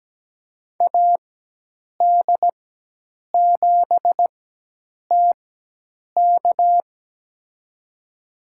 Morse code, heard as a keyed tone, about 17 wpm, spaced as for 8 wpm, 700 Hz. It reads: AD7TK
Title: USA Call Signs - 4x Character Spacing - 17wpm